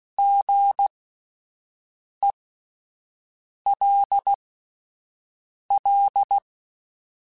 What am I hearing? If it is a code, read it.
Morse code, heard as a keyed tone, about 16 wpm, spaced as for 5 wpm, 800 Hz. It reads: GELL
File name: Spaced Alphas / Spaced Alphas mp3s / Gs sp @10